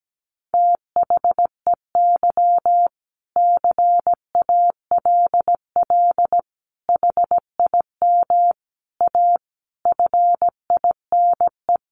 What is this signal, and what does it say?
Morse code, heard as a keyed tone, about 17 wpm, 700 Hz. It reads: THEY CALL HIM A FINE
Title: Sentences from Top 500 Words 17wpm